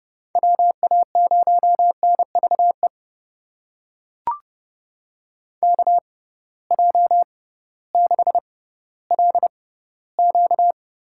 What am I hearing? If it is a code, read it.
Morse code, heard as a keyed tone, about 32 wpm, spaced as for 17 wpm, 700 Hz. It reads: WA0NVE KJ6LQ
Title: USA Call Signs - 6 Characters - 6x Character Spacing - 30wpm